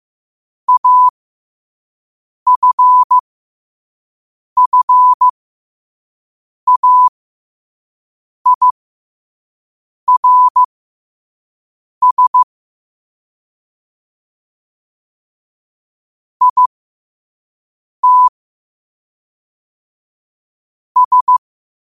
Morse code, heard as a keyed tone, about 15 wpm, spaced as for 5 wpm, 1000 Hz. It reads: AFFAIRS IT S
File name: qotdfiles / 5wpm